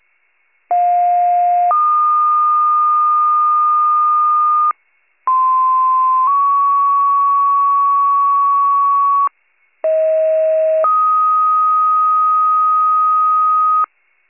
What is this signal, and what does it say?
Two-tone page, heard as a one-second tone, then a three-second tone, about 710 then 1220 Hz, about 1040 then 1120 Hz, about 640 then 1250 Hz. It is heard from 0.7 to 13.9 s.